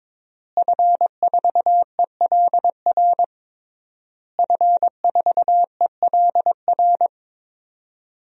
Morse code, heard as a keyed tone, about 22 wpm, 700 Hz. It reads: F4ELR F4ELR
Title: DXCC-100 Call Signs - 5 Characters - Repeated 3x - 22wpm